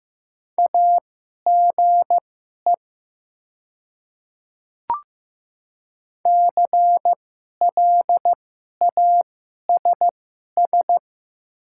Morse code, heard as a keyed tone, about 15 wpm, spaced as for 11 wpm, 700 Hz. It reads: AGE CLASS
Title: 2x Character Spacing - Top 500 Words - 15wpm